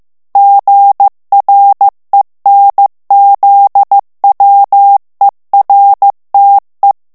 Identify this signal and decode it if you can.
Morse code, heard as a keyed tone, about 15 wpm, 800 Hz. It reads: GRENZWERTE
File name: morse mp3 / ALLEMAND texte clair 1 vitesse 14 mots minute